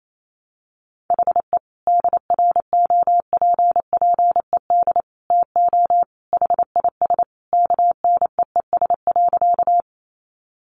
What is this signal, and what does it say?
Morse code, heard as a keyed tone, about 28 wpm, 700 Hz. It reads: HE DROPPED TO HIS KNEES.
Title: A Sound Of Thunder 28wpm